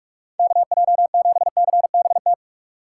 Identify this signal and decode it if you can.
Morse code, heard as a keyed tone, about 44 wpm, 700 Hz. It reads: KJ7CBT
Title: USA Call Signs - 4x Character Spacing - 45wpm